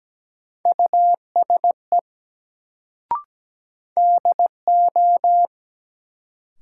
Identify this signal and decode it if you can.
Morse code, heard as a keyed tone, about 17 wpm, 700 Hz. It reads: USE DO
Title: Sentences from Top 100 Words - New Word Drill - 17wpm